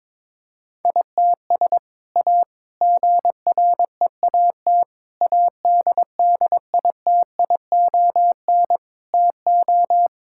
Morse code, heard as a keyed tone, about 21 wpm, 700 Hz. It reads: ITS A GREAT ADDITION TO
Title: Sentences from Top 1000 Words - Review Part 6 - 22wpm